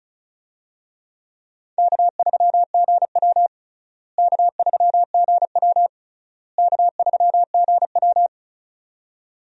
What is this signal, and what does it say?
Morse code, heard as a keyed tone, about 36 wpm, 700 Hz. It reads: K3GW K3GW K3GW
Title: USA Call Signs - 4 Characters - Repeated 3x - 35wpm